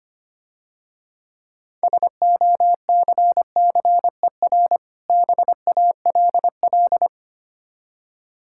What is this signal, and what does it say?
Morse code, heard as a keyed tone, about 25 wpm, 700 Hz. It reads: SOCCER BALL
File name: Easy Words - Speed-Racing with Hints - 25wpm